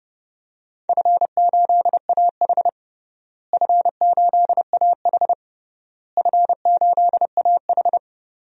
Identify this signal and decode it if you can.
Morse code, heard as a keyed tone, about 32 wpm, 700 Hz. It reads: F8AH F8AH F8AH
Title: DXCC-100 Call Signs - 4 Characters - Repeated 3x - 30wpm